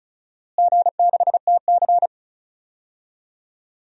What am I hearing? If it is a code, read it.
Morse code, heard as a keyed tone, about 36 wpm, 700 Hz. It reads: G6TC
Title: DXCC-100 Call Signs - 4 Characters - Repeated 2x - 35wpm